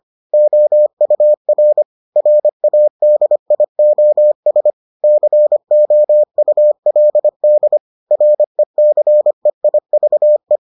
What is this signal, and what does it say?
Morse code, heard as a keyed tone, about 25 wpm, 600 Hz. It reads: OUR RADIOS COULD RECEIVE